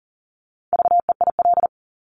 Morse code, heard as a keyed tone, about 42 wpm, 700 Hz. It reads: VEIL
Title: Multi-letter Phonemes Repeated 3x 40wpm